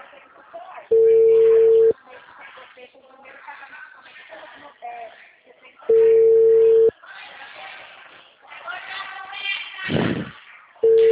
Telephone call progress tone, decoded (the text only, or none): ringback tone